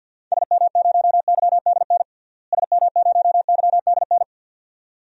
Morse code, heard as a keyed tone, about 47 wpm, 700 Hz. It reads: SM0YDN SM0YDN
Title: DXCC-100 Call Signs - 6 Characters - Repeated 2x - 50wpm